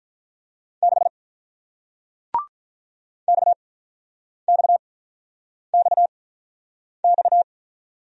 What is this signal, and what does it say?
Morse code, heard as a keyed tone, about 41 wpm, spaced as for 8 wpm, 700 Hz. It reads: 6 XXXX
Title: Speed-Racing Single Letter-Number 35wpm